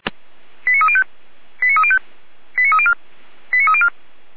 Five-tone selective call (ZVEI), heard as quick runs of five tones, tones of about 70 ms. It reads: 89305 89305 89304 89304
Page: ZVEI